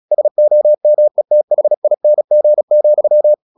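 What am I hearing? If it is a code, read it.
Morse code, heard as a keyed tone, about 36 wpm, 600 Hz. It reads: SOMETHING,
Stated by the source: Morse practice at 36 wpm, text of Ken Follett's novel Never, from CW med Gnister.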